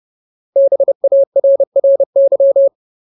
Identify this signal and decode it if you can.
Morse code, heard as a keyed tone, about 30 wpm, 550 Hz. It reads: BARRY
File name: Ultimate Word List / 30WPM Ultimate